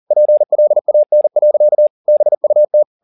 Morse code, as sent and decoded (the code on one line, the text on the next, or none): .--. .-.. .- -. .-.-.- / -... ..- -
PLAN. BUT